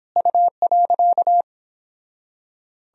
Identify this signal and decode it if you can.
Morse code, heard as a keyed tone, about 29 wpm, 700 Hz. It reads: U.